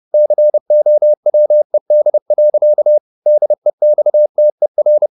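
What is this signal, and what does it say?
Morse code, heard as a keyed tone, about 30 wpm, 600 Hz. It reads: COWED. DEXTER
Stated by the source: Morse code at 30 wpm, the novel Never by Ken Follett in English, from CW med Gnister.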